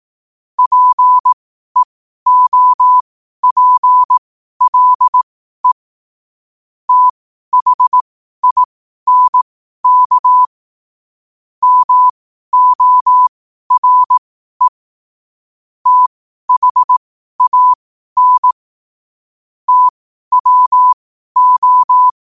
Morse code, heard as a keyed tone, about 18 wpm, spaced as for 12 wpm, 1000 Hz. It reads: PEOPLE THINK MORE THAN TWO
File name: qotdfiles / 12wpm